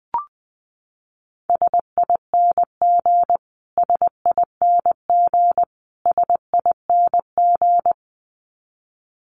Morse code, heard as a keyed tone, about 20 wpm, 700 Hz. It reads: SING SING SING
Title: Top 500 Words Repeated 20wpm